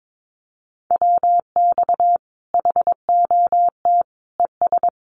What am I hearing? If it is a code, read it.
Morse code, heard as a keyed tone, about 22 wpm, 700 Hz. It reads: WX HOT ES
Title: Common QSO Phrases 22wpm